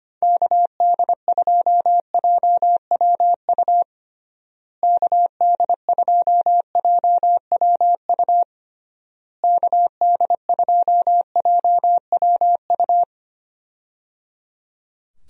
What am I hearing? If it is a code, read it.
Morse code, heard as a keyed tone, about 25 wpm, 700 Hz. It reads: KD2JWU KD2JWU KD2JWU
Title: USA Call Signs - 6 Characters - Repeated 3x - 25wpm